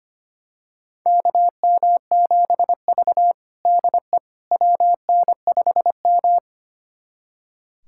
Morse code, heard as a keyed tone, about 25 wpm, 700 Hz. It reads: KM7V DE WN5M
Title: USA Callsign DE Callsign - 4 Characters - 25wpm